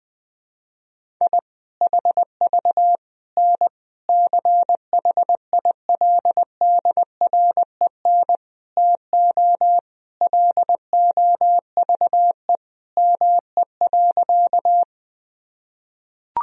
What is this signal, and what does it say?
Morse code, heard as a keyed tone, about 19 wpm, 700 Hz. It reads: I HV N CHILDREN TO LOVE ME.
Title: Sentences from Top 400 Words and Abbreviations 20wpm